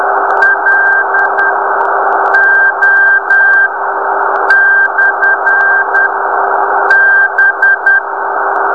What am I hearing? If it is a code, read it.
Morse code, heard as a keyed tone, about 18 wpm, spaced as for 8 wpm, 1600 Hz, with noise. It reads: LO/B
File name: beacon / 2014 02-09 nx0p